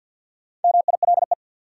Morse code, heard as a keyed tone, about 52 wpm, 700 Hz. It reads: MILE
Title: Sentences from Top 400 Words - New Word Drill - 50wpm